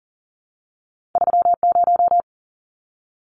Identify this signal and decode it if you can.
Morse code, heard as a keyed tone, about 40 wpm, 700 Hz. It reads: 30
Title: Numbers - 2-Digit 40wpm